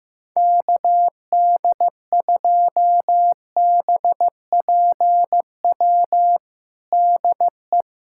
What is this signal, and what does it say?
Morse code, heard as a keyed tone, about 15 wpm, 700 Hz. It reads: KD2BPW DE
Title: USA Callsign DE Callsign 15wpm